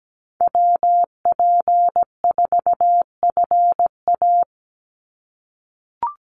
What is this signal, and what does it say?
Morse code, heard as a keyed tone, about 17 wpm, 700 Hz. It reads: WP4FA
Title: USA Call Signs 17wpm